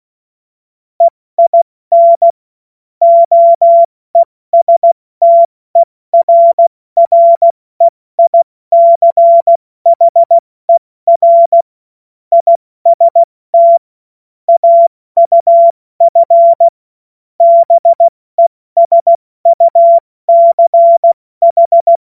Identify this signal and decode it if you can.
Morse code, heard as a keyed tone, about 16 wpm, 680 Hz. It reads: EIN OESTERREICHER IST AUF BESUCH
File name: Morsekurs / Witze / Witz 07.November 2023 70 F80